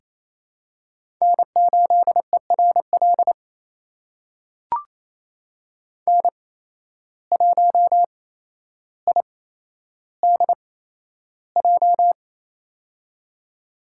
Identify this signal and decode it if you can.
Morse code, heard as a keyed tone, about 28 wpm, spaced as for 12 wpm, 700 Hz. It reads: N8ERL N1IDJ
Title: USA Call Signs - 8x Character Spacing - 28wpm